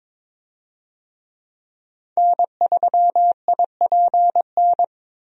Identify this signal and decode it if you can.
Morse code, heard as a keyed tone, about 21 wpm, 700 Hz. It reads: N3IPN
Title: USA Call Signs - 5 Characters - Repeated 3x - 22wpm